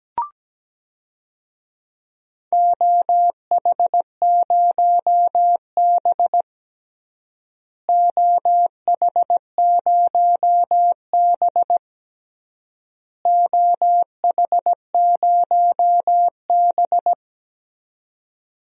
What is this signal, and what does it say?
Morse code, heard as a keyed tone, about 17 wpm, 700 Hz. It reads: OH0B OH0B OH0B
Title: DXCC-100 Call Signs - 4 Characters - Repeated 3x - 17wpm